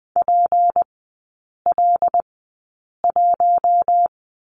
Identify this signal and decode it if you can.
Morse code, heard as a keyed tone, about 20 wpm, spaced as for 8 wpm, 700 Hz. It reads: PL1
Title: 152 Lesson 25e - Letter p - Three Characters 20wpm